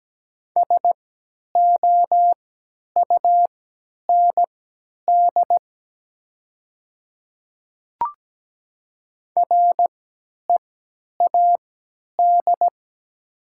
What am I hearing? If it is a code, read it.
Morse code, heard as a keyed tone, about 17 wpm, spaced as for 10 wpm, 700 Hz. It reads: SOUND READ
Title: 3x Character Spacing - Top 200 Words - 17wpm